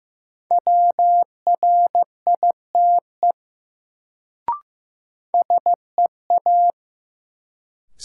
Morse code, heard as a keyed tone, about 15 wpm, 700 Hz. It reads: WRITE SEA
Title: Sentences from Top 300 Words - New Word Drill - 15wpm